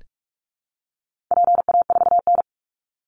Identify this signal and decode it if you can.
Morse code, heard as a keyed tone, about 44 wpm, 700 Hz. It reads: PA4N
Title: Call Signs Repeated 2x - 45wpm